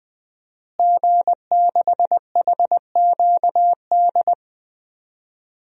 Morse code, heard as a keyed tone, about 20 wpm, 700 Hz. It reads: G6HQD
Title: DXCC-100 Call Signs - 5 Characters - Repeated 3x - 20wpm